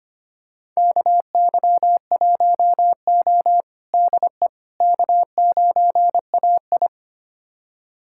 Morse code, heard as a keyed tone, about 25 wpm, 700 Hz. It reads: KY1O DE K9AI
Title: USA Callsign DE Callsign - 4 Characters - 25wpm